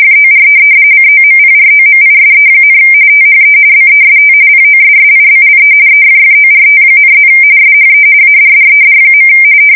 RTTY, about 45.45 baud. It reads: FOR COMPROMISE, THE BEST POSSIBLE EFFORT,+ SHE ADDED.
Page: 170 Shift 2125/2295